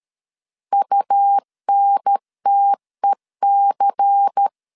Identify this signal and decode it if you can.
Morse code, heard as a keyed tone, about 13 wpm, 800 Hz. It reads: UNTEC
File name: LEZIONI / 7 GF9Z / 7 GF9 12wpm 2319